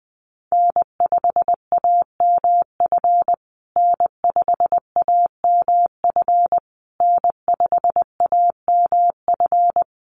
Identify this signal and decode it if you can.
Morse code, heard as a keyed tone, about 20 wpm, 700 Hz. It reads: N5AMF N5AMF N5AMF
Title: USA Call Signs Repeated 3x 20wpm